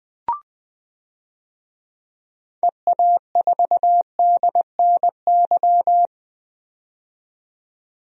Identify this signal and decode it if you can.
Morse code, heard as a keyed tone, about 20 wpm, 700 Hz. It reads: EA4DNY
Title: DXCC-100 Call Signs - 6 Characters - 20wpm